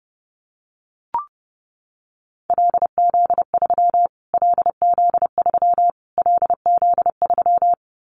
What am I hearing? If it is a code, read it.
Morse code, heard as a keyed tone, about 28 wpm, 700 Hz. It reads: LZ3 LZ3 LZ3
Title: Callsign Prefixes for Top 25 DXCC - Repeated 3x - 30wpm